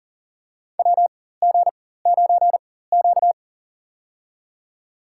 Morse code, heard as a keyed tone, about 39 wpm, spaced as for 18 wpm, 700 Hz. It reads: WG9Q
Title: USA Call Signs - 4x Character Spacing - 40wpm